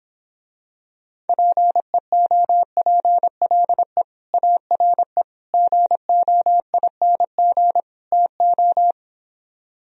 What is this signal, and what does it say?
Morse code, heard as a keyed tone, about 26 wpm, 700 Hz. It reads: PEOPLE ARE GOING TO